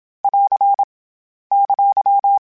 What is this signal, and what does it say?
Morse code, heard as a keyed tone, about 27 wpm, 800 Hz. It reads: <AR> !
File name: Short Runs / Short Runs-mp3s / Words @25